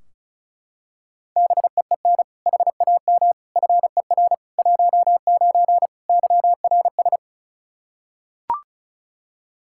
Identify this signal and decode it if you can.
Morse code, heard as a keyed tone, about 34 wpm, 700 Hz. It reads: BEEN HAM FER 19 YRS
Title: Practice QSO - Exchange 3 - More Info - 35wpm